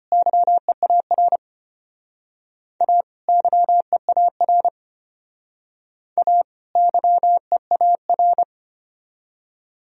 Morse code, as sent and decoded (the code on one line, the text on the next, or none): -.-- . .- .-. / .- / -.-- . .- .-. / .- / -.-- . .- .-.
YEAR A YEAR A YEAR